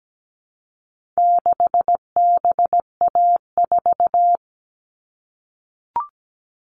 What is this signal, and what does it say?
Morse code, heard as a keyed tone, about 17 wpm, 700 Hz. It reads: 6BA4
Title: All Cut Numbers - 4-Digit 17wpm